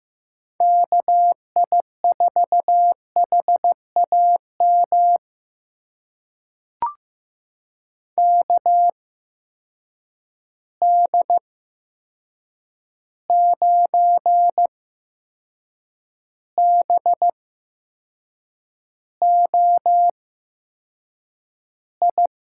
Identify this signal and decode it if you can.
Morse code, heard as a keyed tone, about 15 wpm, spaced as for 6 wpm, 700 Hz. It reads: KI4HAM KD9BOI
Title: USA Call Signs - 6 Characters - 8x Character Spacing - 15wpm